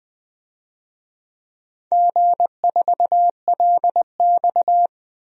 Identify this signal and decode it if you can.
Morse code, heard as a keyed tone, about 20 wpm, 700 Hz. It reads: G4LX
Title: DXCC-100 Call Signs - 4 Characters - 20wpm